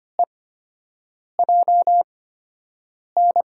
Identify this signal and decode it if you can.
Morse code, heard as a keyed tone, about 25 wpm, spaced as for 7 wpm, 700 Hz. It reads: EJN
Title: USA Call Signs - 5 Characters - 8x Character Spacing - 25wpm